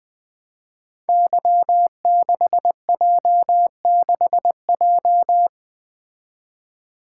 Morse code, heard as a keyed tone, about 20 wpm, 700 Hz. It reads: Y6J6J
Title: Code Groups - 5 Letter:Numbers - 20wpm